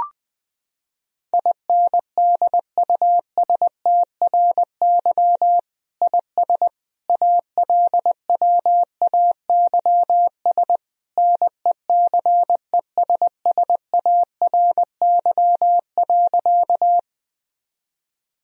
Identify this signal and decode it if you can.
Morse code, heard as a keyed tone, about 20 wpm, 700 Hz. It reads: INDUSTRY IS ALWAYS NECESSARY.
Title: Sentences from Top 1000 Words - Review Part 4 - 20wpm